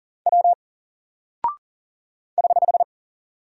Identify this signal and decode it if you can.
Morse code, heard as a keyed tone, about 40 wpm, 700 Hz. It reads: W <HH>
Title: Single Character (Letter:Number:Punctuation:Prosign) Alt 40wpm